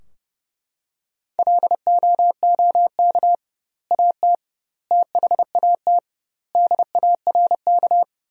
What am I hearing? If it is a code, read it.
Morse code, heard as a keyed tone, about 30 wpm, 700 Hz. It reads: LOOK AT THAT DARK
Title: 2x Word-Spacing - Sentences from Top 500 Words - 30wpm